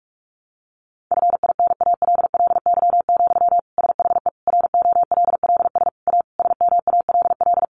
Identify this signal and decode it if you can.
Morse code, heard as a keyed tone, about 44 wpm, 700 Hz. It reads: FINALLY, SHE ROLLS A SMALL
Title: Easy Phrases 45wpm